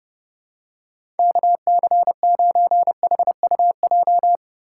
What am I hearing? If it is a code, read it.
Morse code, heard as a keyed tone, about 30 wpm, 700 Hz. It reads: KC9HUJ
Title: USA Call Signs - 6 Characters - 4x Character Spacing - 30wpm